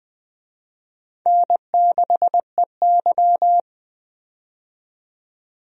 Morse code, heard as a keyed tone, about 20 wpm, 700 Hz. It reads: N6EY
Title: USA Call Signs - 4 Characters - 20wpm